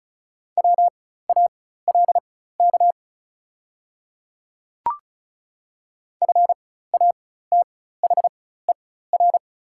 Morse code, heard as a keyed tone, about 36 wpm, spaced as for 16 wpm, 690 Hz. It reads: WALK FATHER
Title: 4x Character Spacing - Top 300 Words - 35wpm